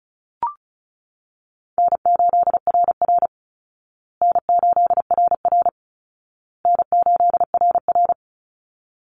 Morse code, heard as a keyed tone, about 34 wpm, 700 Hz. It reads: N8RR N8RR N8RR
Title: Call Signs Repeated 2x - 35wpm